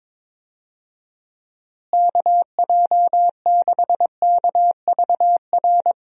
Morse code, heard as a keyed tone, about 22 wpm, 700 Hz. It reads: KJ6KVR
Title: USA Call Signs - 6 Characters - Repeated 2x - 22wpm